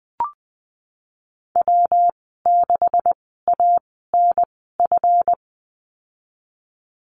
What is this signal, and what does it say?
Morse code, heard as a keyed tone, about 20 wpm, spaced as for 14 wpm, 700 Hz. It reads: W6ANF
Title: 154 Lesson 25g - Letter p - Call Signs 20wpm